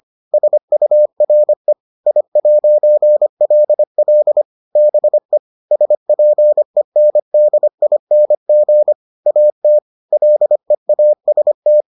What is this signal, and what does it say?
Morse code, heard as a keyed tone, about 25 wpm, 600 Hz. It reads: SURE I'LL BE SPENDING AT LEAST